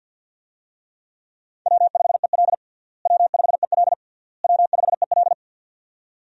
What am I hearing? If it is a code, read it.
Morse code, heard as a keyed tone, about 53 wpm, 700 Hz. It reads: W5EL W5EL W5EL
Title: USA Call Signs - 4 Characters - Repeated 3x - 50wpm